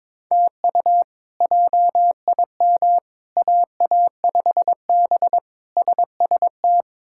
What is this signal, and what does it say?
Morse code, heard as a keyed tone, about 21 wpm, 700 Hz. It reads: TU JIM AA5B SST
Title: K1USN SST QSO Practice - 22wpm